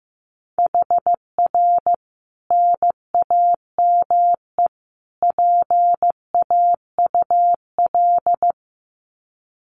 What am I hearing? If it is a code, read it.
Morse code, heard as a keyed tone, about 15 wpm, 700 Hz. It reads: HR NAME PAUL
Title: Common QSO Phrases 15wpm